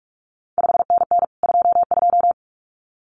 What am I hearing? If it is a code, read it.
Morse code, heard as a keyed tone, about 46 wpm, 700 Hz. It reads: 5NN 22
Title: CQ WW DX Contest Exchange Encore - 45wpm